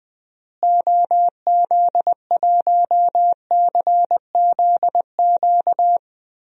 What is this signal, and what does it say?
Morse code, heard as a keyed tone, about 20 wpm, 700 Hz. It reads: OZ1CZQ
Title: DXCC-100 Call Signs - 6 Characters - Repeated 2x - 20wpm